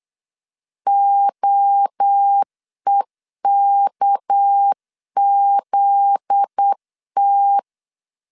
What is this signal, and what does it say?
Morse code, heard as a keyed tone, about 8 wpm, 800 Hz. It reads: OEKZT
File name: LEZIONI / 8 YJ45 / 8 Y 8wpm 3245